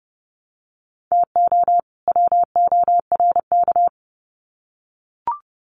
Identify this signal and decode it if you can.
Morse code, heard as a keyed tone, about 30 wpm, 700 Hz. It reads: TO WORK
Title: Sets of 2 Words 30wpm